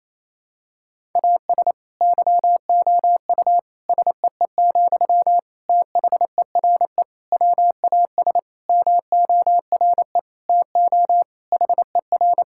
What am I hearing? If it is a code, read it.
Morse code, heard as a keyed tone, about 28 wpm, 700 Hz. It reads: AS YOU SEE, THERE WAS MORE TO HER
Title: Sentences from Top 1000 Words - Review Part 6 - 28wpm